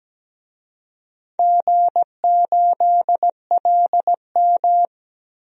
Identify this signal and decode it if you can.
Morse code, heard as a keyed tone, about 17 wpm, 700 Hz. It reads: G8LM
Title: DXCC-100 Call Signs - 4 Characters - Repeated 3x - 17wpm